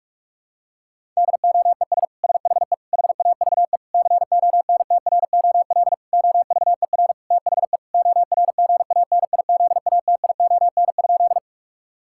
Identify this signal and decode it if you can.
Morse code, heard as a keyed tone, about 41 wpm, 700 Hz. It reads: DOES SHE HAVE CONTROL OVER THE ORGANIZATION?
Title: Sentences from Top 1000 Words - Review Part 5 - 45wpm